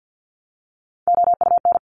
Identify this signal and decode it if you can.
Morse code, heard as a keyed tone, about 50 wpm, 700 Hz. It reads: OUN
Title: Top Three Letter Combinations in Words 50wpm